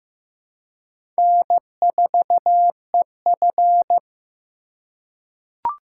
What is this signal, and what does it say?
Morse code, heard as a keyed tone, about 15 wpm, 700 Hz. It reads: N4EF
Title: USA Call Signs - 4 Characters - 4x Character Spacing - 15wpm